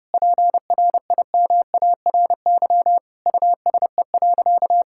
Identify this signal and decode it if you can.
Morse code, heard as a keyed tone, about 30 wpm, 700 Hz. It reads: PRIMARY USE.
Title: Sentences from Top 1000 Words - Review Part 5 - 30wpm